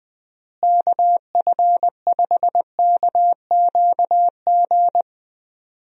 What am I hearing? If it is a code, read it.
Morse code, heard as a keyed tone, about 20 wpm, 700 Hz. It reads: KF5KQG
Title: USA Call Signs - 6 Characters - Repeated 2x - 20wpm